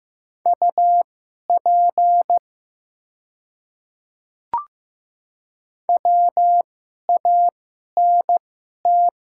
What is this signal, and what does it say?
Morse code, heard as a keyed tone, about 15 wpm, spaced as for 11 wpm, 700 Hz. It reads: UP WANT
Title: 2x Character Spacing - Top 200 Words - 15wpm